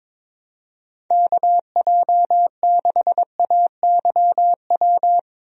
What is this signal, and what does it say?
Morse code, heard as a keyed tone, about 22 wpm, 700 Hz. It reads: KJ6AYW
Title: USA Call Signs - 6 Characters - Repeated 3x - 22wpm